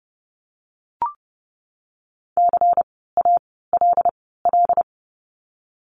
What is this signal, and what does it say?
Morse code, heard as a keyed tone, about 30 wpm, spaced as for 17 wpm, 700 Hz. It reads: CALL
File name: course / 30:10wpm / 108 Lesson 18f - Letter m - Words 30wpm